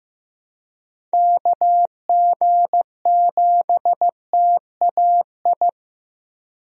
Morse code, heard as a keyed tone, about 15 wpm, 700 Hz. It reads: KG7TAI
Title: USA Call Signs - 6 Characters - Repeated 3x - 15wpm